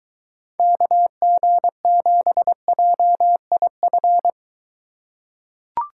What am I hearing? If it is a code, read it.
Morse code, heard as a keyed tone, about 23 wpm, 700 Hz. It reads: KG7JIF